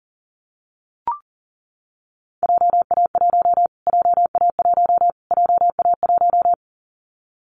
Morse code, heard as a keyed tone, about 40 wpm, 700 Hz. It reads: JA1 JA1 JA1
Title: Callsign Prefixes for Top 25 DXCC - Repeated 3x - 40wpm